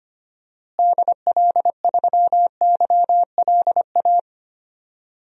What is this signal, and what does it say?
Morse code, heard as a keyed tone, about 25 wpm, 700 Hz. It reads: DL3YLA
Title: DXCC-100 Call Signs - 6 Characters - Repeated 2x - 25wpm